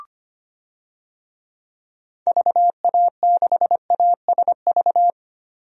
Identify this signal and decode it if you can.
Morse code, heard as a keyed tone, about 25 wpm, 700 Hz. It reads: VA6ASV
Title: DXCC-100 Call Signs - 6 Characters - Repeated 2x - 25wpm